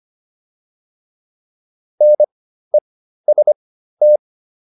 Morse code, heard as a keyed tone, about 25 wpm, spaced as for 13 wpm, 600 Hz. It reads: NEST